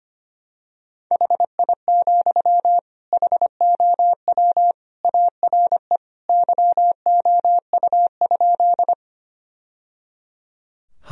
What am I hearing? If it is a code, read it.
Morse code, heard as a keyed tone, about 25 wpm, 700 Hz. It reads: HI, HOW ARE YOU?
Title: Sentences from Top 1000 Words - Review Part 3 - 25wpm